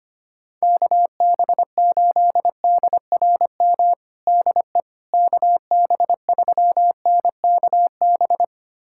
Morse code, heard as a keyed tone, about 25 wpm, 700 Hz. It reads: KB8DRM DE KB3NKB
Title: USA Callsign DE Callsign - 6 Characters - 25wpm